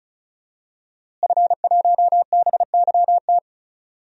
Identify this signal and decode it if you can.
Morse code, heard as a keyed tone, about 36 wpm, 700 Hz. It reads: F1BYT
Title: DXCC-100 Call Signs - 5 Characters - Repeated 3x - 35wpm